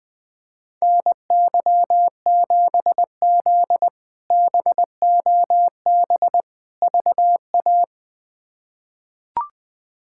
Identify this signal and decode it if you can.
Morse code, heard as a keyed tone, about 20 wpm, 700 Hz. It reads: NY7Z BOB VA
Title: K1USN SST QSO Practice - 20wpm